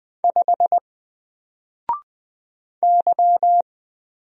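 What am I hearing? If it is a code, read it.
Morse code, heard as a keyed tone, about 20 wpm, 700 Hz. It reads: 5 Y
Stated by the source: Single letter and number Morse drill at 20 wpm, from N4LCD.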